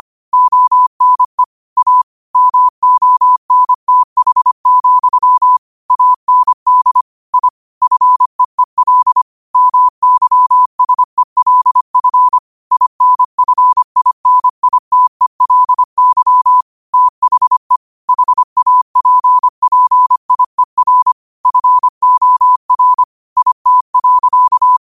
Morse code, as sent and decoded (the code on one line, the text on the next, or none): --- -. / .- / -- --- -. - .... --..-- / .- -. -.. / .. / ..-. . .-.. / -- -.-- ... .-.. ..-. / .. -. ..-. .. -. .. - . .-.. -.-- / - .... / .... .- .--. .--. .. .-. / ..-. --- .-. / .. - .-.-.-
ONE A MONTH, AND I FEEL MYSELF INFINITELY THE HAPPIER FOR IT.